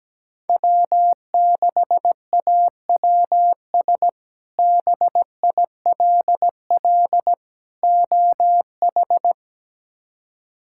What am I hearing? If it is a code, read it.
Morse code, heard as a keyed tone, about 17 wpm, 700 Hz. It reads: W6AWS BILL OH
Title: K1USN SST QSO Practice - 17wpm